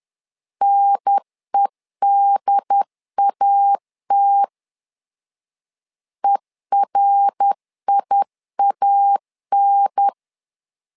Morse code, heard as a keyed tone, about 10 wpm, 800 Hz. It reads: NEDAT ERIAN
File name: LEZIONI / 3 MWRO / 3 MWR 10wpm 1952